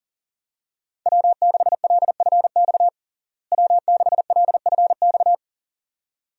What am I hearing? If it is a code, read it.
Morse code, heard as a keyed tone, about 40 wpm, 700 Hz. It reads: W6LFX W6LFX
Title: USA Call Signs - 5 Characters - Repeated 2x - 40wpm